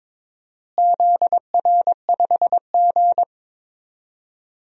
Morse code, heard as a keyed tone, about 22 wpm, 700 Hz. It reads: ZR5G
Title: DXCC-100 Call Signs - 4 Characters - Repeated 3x - 22wpm